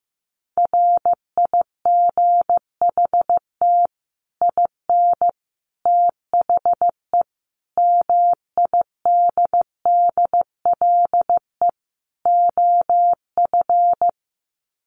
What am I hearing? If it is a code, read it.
Morse code, heard as a keyed tone, about 15 wpm, 700 Hz. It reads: RIGHT IN THE MIDDLE OF
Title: Sets of 5 Words 15wpm